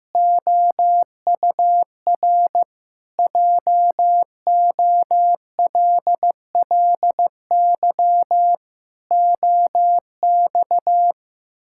Morse code, heard as a keyed tone, about 15 wpm, 700 Hz. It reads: OUR JOLLY OX
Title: Pangram Challenge v2 - 15wpm